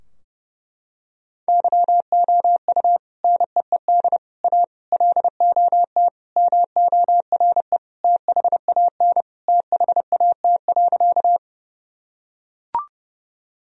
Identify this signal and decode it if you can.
Morse code, heard as a keyed tone, about 30 wpm, 700 Hz. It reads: YOU NEED A LOT MORE THAN THAT.
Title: Sentences from Top 1000 Words - Review Part 3 - 30wpm